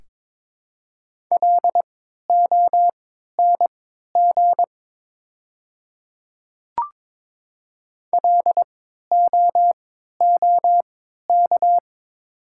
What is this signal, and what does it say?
Morse code, heard as a keyed tone, about 22 wpm, spaced as for 12 wpm, 690 Hz. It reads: LONG LOOK
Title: 3x Character Spacing - Top 200 Words - 22wpm